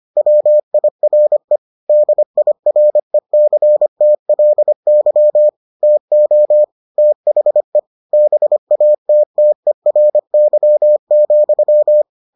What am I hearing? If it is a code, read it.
Morse code, heard as a keyed tone, about 25 wpm, 600 Hz. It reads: WIRE DIRECTLY TO THE BATTERY,